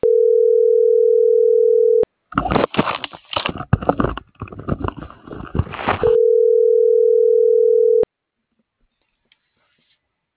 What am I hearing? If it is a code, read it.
Ringback tone (North America), 440 + 480 Hz.